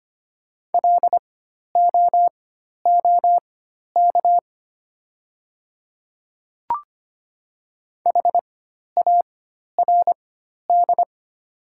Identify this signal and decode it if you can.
Morse code, heard as a keyed tone, about 25 wpm, spaced as for 12 wpm, 690 Hz. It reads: LOOK HARD
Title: 4x Character Spacing - Top 300 Words - 25wpm